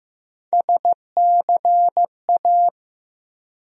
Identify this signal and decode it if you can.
Morse code, heard as a keyed tone, about 15 wpm, 700 Hz. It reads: SCA
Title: USA Call Sign Postfixes - 3 Letters 15wpm